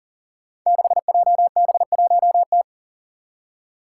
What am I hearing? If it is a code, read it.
Morse code, heard as a keyed tone, about 41 wpm, 700 Hz. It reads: 6JB1T
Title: Five Challenging Letter-Number Combinations 40wpm